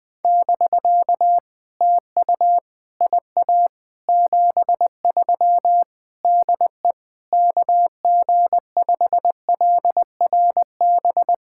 Morse code, heard as a keyed tone, about 20 wpm, 700 Hz. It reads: <BK> TU IA 73 DE KG5LRB
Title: POTA QSO Practice - 20wpm